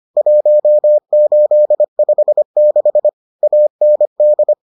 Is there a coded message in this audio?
Yes: Morse code from 0.2 to 4.5 s, heard as a keyed tone, about 25 wpm, 600 Hz. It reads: 1856 AND